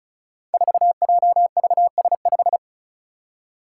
Morse code, heard as a keyed tone, about 36 wpm, 700 Hz. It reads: 4JVS5
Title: Five Challenging Letter-Number Combinations 35wpm